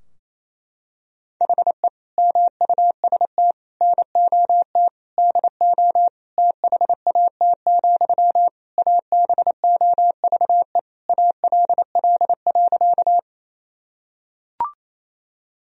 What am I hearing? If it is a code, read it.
Morse code, heard as a keyed tone, about 28 wpm, 700 Hz. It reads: HE MUST NOT DO THAT, ABOVE ALL.
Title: Sentences from Top 1000 Words - Review Part 2 - 28wpm